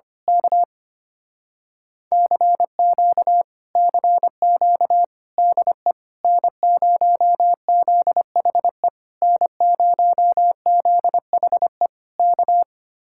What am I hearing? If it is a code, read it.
Morse code, heard as a keyed tone, about 24 wpm, 700 Hz. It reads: K CQ CQ DE N0ZHE N0ZHE K
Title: Speed-Racing Common QSO Phrases Encore - 25wpm